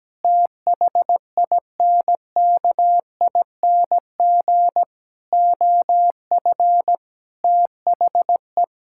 Morse code, heard as a keyed tone, about 17 wpm, 700 Hz. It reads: THINKING OF THE
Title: Sentences from Top 1000 Words - Review Part 6 - 17wpm